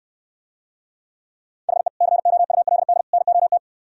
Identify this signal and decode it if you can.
Morse code, heard as a keyed tone, about 67 wpm, 700 Hz. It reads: HE COULD NOT